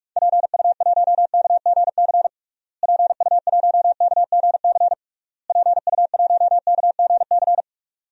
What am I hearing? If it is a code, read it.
Morse code, heard as a keyed tone, about 46 wpm, 700 Hz. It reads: PU1KGC PU1KGC PU1KGC
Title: DXCC-100 Call Signs - 6 Characters - Repeated 3x - 45wpm